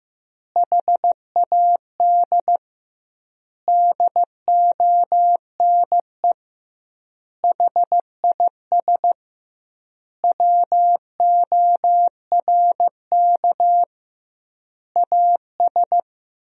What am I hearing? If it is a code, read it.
Morse code, heard as a keyed tone, about 16 wpm, 700 Hz. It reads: HAD DONE HIS WORK AS
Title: 2x Word-Spacing - Sentences from Top 500 Words - 15wpm